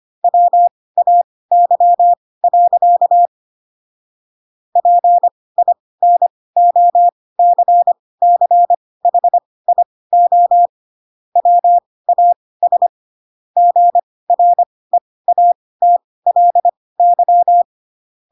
Morse code, heard as a keyed tone, about 25 wpm, spaced as for 18 wpm, 700 Hz. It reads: WAY. PINOCCHIO WAS GREATLY
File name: Pinocchio / Pinocchio ch6 12 25 ik1hge